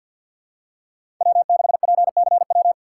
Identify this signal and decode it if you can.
Morse code, heard as a keyed tone, about 49 wpm, 700 Hz. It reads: W6PCW
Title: USA Call Signs - 5 Characters - 3x Character Spacing - 50wpm